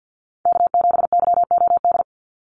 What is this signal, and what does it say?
Morse code, heard as a keyed tone, about 50 wpm, 700 Hz. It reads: K7YOD
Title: USA Call Signs 50wpm